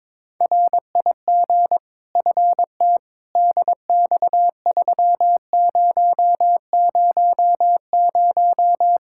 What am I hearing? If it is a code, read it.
Morse code, heard as a keyed tone, about 22 wpm, 700 Hz. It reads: RIG FT DX3000
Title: Practice QSO - Exchange 2 - Info - 22wpm